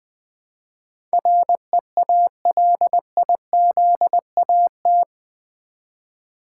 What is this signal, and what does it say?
Morse code, heard as a keyed tone, about 20 wpm, 700 Hz. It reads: REALIZAT
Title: Build A Word - 4 Letters At A Time - 11-Letter Words - 20wpm